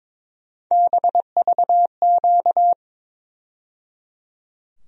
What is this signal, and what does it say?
Morse code, heard as a keyed tone, about 22 wpm, 700 Hz. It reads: BVQ
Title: USA Call Sign Postfixes - 3 Letters 22wpm